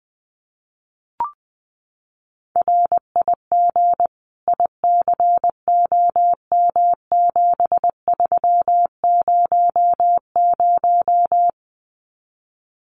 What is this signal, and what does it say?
Morse code, heard as a keyed tone, about 20 wpm, 700 Hz. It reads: RIG ICOM7300
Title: Common QSO Phrases 20wpm